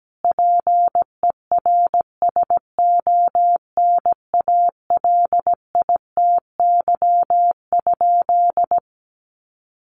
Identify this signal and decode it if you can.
Morse code, heard as a keyed tone, about 17 wpm, 700 Hz. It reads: PERSONALITY?
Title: Sentences from Top 1400 Words 17wpm